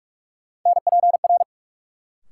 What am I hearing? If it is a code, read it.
Morse code, heard as a keyed tone, about 46 wpm, 700 Hz. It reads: NPR
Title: Sets of 5 Words - New Word Drill - 45wpm